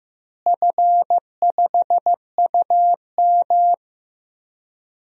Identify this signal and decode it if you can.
Morse code, heard as a keyed tone, about 16 wpm, 700 Hz. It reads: F5UM
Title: DXCC-100 Call Signs - 4 Characters - Repeated 3x - 15wpm